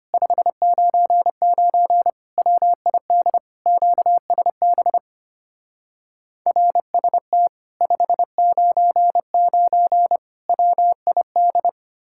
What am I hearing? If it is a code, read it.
Morse code, heard as a keyed tone, about 33 wpm, 700 Hz. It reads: 599 WID QSB RST 599 WID